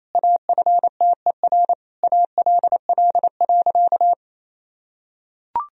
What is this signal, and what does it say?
Morse code, heard as a keyed tone, about 28 wpm, 700 Hz. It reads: AFTER ALL.
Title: Sentences from Top 1000 Words - Review Part 2 - 28wpm